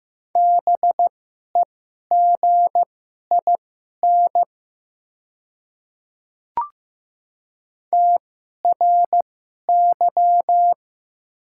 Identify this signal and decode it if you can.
Morse code, heard as a keyed tone, about 15 wpm, spaced as for 11 wpm, 690 Hz. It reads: BEGIN TRY
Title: 2x Character Spacing - Top 500 Words - 15wpm